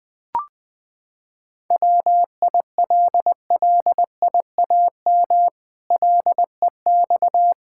Morse code, heard as a keyed tone, about 20 wpm, 700 Hz. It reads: WILLIAM LEX
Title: Pangram Challenge v2 - 20wpm